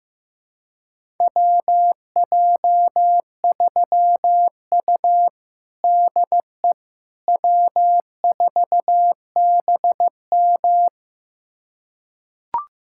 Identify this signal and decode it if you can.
Morse code, heard as a keyed tone, about 16 wpm, 700 Hz. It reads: WJ3U DE W4BM
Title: USA Callsign DE Callsign - 4 Characters - 15wpm